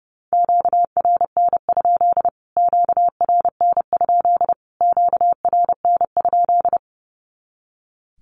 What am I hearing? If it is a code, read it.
Morse code, heard as a keyed tone, about 30 wpm, 700 Hz. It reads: QRN? QRN? QRN?
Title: Q-Codes Repeated 3x 30wpm